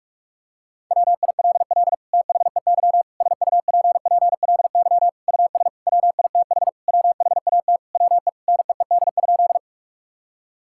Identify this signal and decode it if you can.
Morse code, heard as a keyed tone, about 46 wpm, 700 Hz. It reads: WILL THEY SUPPLY US WITH WHAT WE NEED?
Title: Sentences from Top 1000 Words - Review Part 5 - 45wpm